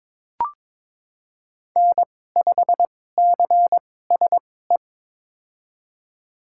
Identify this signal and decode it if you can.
Morse code, heard as a keyed tone, about 21 wpm, spaced as for 16 wpm, 700 Hz. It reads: N5CSE
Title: USA Call Signs - 5 Characters - 2x Character Spacing - 22wpm